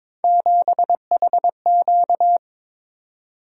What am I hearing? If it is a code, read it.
Morse code, heard as a keyed tone, about 22 wpm, 700 Hz. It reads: 7HQ
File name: practice / 22wpm / Three Letter-Number Combinations - 22wpm